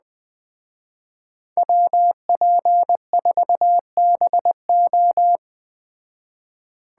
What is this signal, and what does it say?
Morse code, heard as a keyed tone, about 20 wpm, 700 Hz. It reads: WP4BO
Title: Speed Racing - USA Call Signs 17wpm